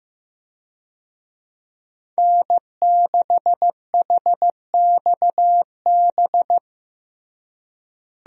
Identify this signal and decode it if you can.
Morse code, heard as a keyed tone, about 15 wpm, 700 Hz. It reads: N6HXB
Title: USA Call Signs - 5 Characters - Repeated 3x - 15wpm